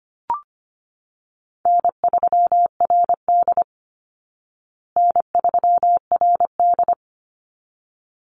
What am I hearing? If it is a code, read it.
Morse code, heard as a keyed tone, about 25 wpm, 700 Hz. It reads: N3RD N3RD
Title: Call Signs Repeated 3x - 25wpm